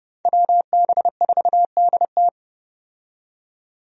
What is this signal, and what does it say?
Morse code, heard as a keyed tone, about 30 wpm, 700 Hz. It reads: WB4DT
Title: USA Call Signs - 5 Characters - 30wpm